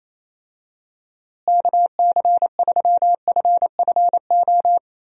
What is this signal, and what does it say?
Morse code, heard as a keyed tone, about 28 wpm, 700 Hz. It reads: KC3FFO
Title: USA Call Signs - 6 Characters - Repeated 3x - 28wpm